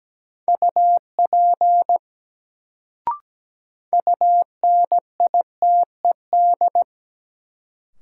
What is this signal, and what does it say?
Morse code, heard as a keyed tone, about 17 wpm, 700 Hz. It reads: UP UNITED
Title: Sets of 2 Words - New Word Drill - 17wpm